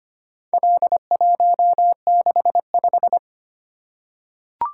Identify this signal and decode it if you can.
Morse code, heard as a keyed tone, about 25 wpm, 700 Hz. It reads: L165
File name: practice / 25wpm / Four Letter-Number Combinations - 25wpm